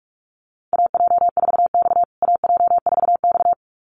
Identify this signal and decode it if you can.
Morse code, heard as a keyed tone, about 44 wpm, 700 Hz. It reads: AJ4X AJ4X
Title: USA Call Signs Repeated 2x 45wpm